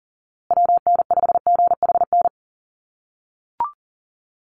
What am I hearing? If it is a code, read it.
Morse code, heard as a keyed tone, about 38 wpm, 700 Hz. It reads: WN5GHN
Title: USA Call Signs Repeated 2x 40wpm